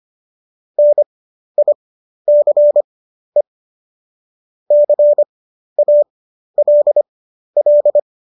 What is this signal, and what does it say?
Morse code, heard as a keyed tone, about 25 wpm, spaced as for 12 wpm, 590 Hz. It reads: NICE CALL